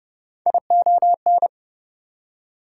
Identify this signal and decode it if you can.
Morse code, heard as a keyed tone, about 29 wpm, 700 Hz. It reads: ION
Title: Prefixes and Suffixes 30wpm